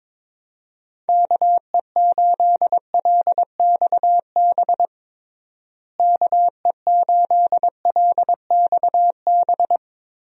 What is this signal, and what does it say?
Morse code, heard as a keyed tone, about 22 wpm, 700 Hz. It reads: KE8LXB KE8LXB
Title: USA Call Signs - 6 Characters - Repeated 2x - 22wpm